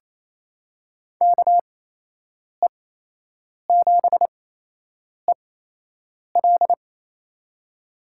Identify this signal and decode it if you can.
Morse code, heard as a keyed tone, about 28 wpm, spaced as for 8 wpm, 700 Hz. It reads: KE7EL
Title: USA Call Signs - 5 Characters - 8x Character Spacing - 28wpm